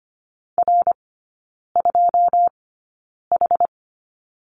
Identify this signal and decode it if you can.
Morse code, heard as a keyed tone, about 25 wpm, spaced as for 9 wpm, 700 Hz. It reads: R2H
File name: Lesson 14e - Number 2 - Three Characters 25wpm